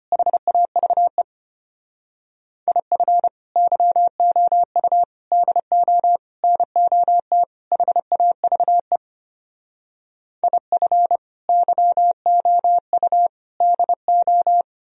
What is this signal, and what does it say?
Morse code, heard as a keyed tone, about 33 wpm, 700 Hz. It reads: HAVE IF YOU DO NOT HAVE IF YOU DO